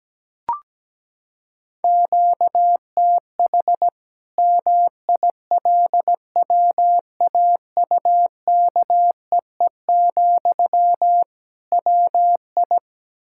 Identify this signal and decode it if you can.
Morse code, heard as a keyed tone, about 17 wpm, 700 Hz. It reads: QTH MILWAUKEE, WI
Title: Practice QSO - Exchange 1 - Beginning - 17wpm